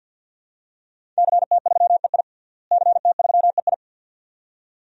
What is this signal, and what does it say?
Morse code, heard as a keyed tone, about 52 wpm, 700 Hz. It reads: CT3EI CT3EI
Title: DXCC-100 Call Signs - 5 Characters - Repeated 2x - 50wpm